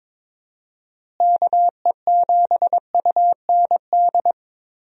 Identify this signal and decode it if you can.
Morse code, heard as a keyed tone, about 22 wpm, 700 Hz. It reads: KE7UND
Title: USA Call Signs - 6 Characters - Repeated 3x - 22wpm